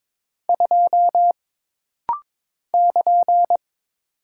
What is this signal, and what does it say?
Morse code, heard as a keyed tone, about 22 wpm, 700 Hz. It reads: 2 <KN>
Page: This rapid-fire timing is geared to push your ICR (Instant Character Recognition) skills.